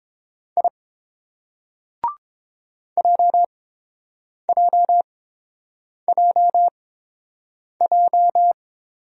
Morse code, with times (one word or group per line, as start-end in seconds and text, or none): 0.6-0.7 I
3.0-8.5 JJJJ